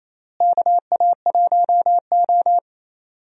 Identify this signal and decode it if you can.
Morse code, heard as a keyed tone, about 28 wpm, 700 Hz. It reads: KA1O
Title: USA Call Signs - 4 Characters - Repeated 3x - 28wpm